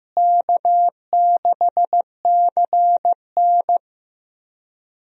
Morse code, heard as a keyed tone, about 16 wpm, 700 Hz. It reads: K6CN
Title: USA Call Signs - 4 Characters - Repeated 3x - 15wpm